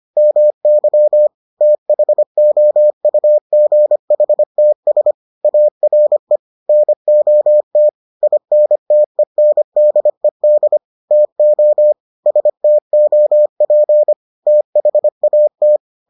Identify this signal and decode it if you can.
Morse code, heard as a keyed tone, about 25 wpm, 600 Hz. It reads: MY THOUGHTS ARE NOT INTENDED TO STOP THAT